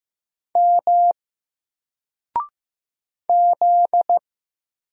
Morse code, heard as a keyed tone, about 15 wpm, 700 Hz. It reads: M Z
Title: Single Letter-Number 15wpm